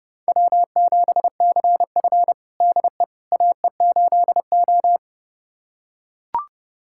Morse code, heard as a keyed tone, about 32 wpm, 700 Hz. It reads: W7CF DE AE8O
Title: USA Callsign DE Callsign - 4 Characters - 30wpm